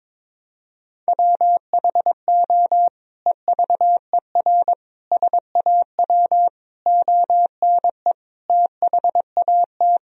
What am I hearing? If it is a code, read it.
Morse code, heard as a keyed tone, about 21 wpm, 700 Hz. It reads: WHO EVER SAW ONE THAT
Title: Sentences from Top 1000 Words - Review Part 2 - 22wpm